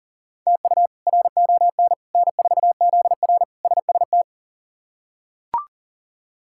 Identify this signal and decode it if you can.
Morse code, heard as a keyed tone, about 40 wpm, 700 Hz. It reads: TU RON N4ZR SST
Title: K1USN SST QSO Practice - 40wpm